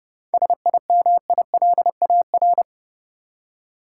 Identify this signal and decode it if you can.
Morse code, heard as a keyed tone, about 32 wpm, 700 Hz. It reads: SIMILAR
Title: Seven Letter Words - 30wpm